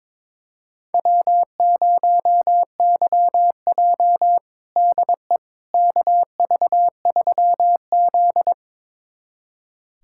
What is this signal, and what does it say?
Morse code, heard as a keyed tone, about 22 wpm, 700 Hz. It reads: W0YJ DE KV3Z
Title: USA Callsign DE Callsign - 4 Characters - 22wpm